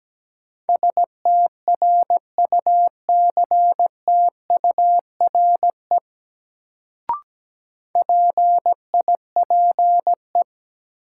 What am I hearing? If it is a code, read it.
Morse code, heard as a keyed tone, about 17 wpm, 700 Hz. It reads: STRUCTURE PIPE
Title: Sentences from Top 1500 Words - New Word Drill Alt - 17wpm